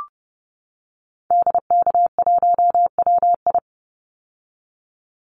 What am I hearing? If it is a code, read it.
Morse code, heard as a keyed tone, about 30 wpm, 700 Hz. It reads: DK1WI
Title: Call Signs - 30wpm